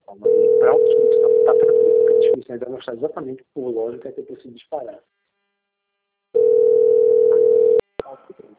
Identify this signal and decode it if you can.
Ringback tone (North America), 440 + 480 Hz, with speech.
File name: out- - - - - .2887